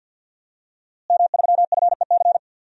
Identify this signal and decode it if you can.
Morse code, heard as a keyed tone, about 46 wpm, 700 Hz. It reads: M3FEC
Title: DXCC-100 Call Signs - 5 Characters - Repeated 2x - 50wpm